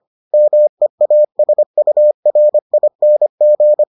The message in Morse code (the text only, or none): MEASURING